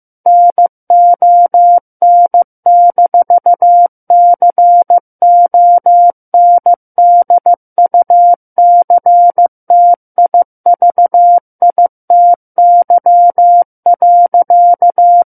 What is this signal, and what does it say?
Morse code, heard as a keyed tone, about 15 wpm, 700 Hz. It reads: NON-CONDUCTIVITY.
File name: Book One / 06-Chapter 6 - 15 WPM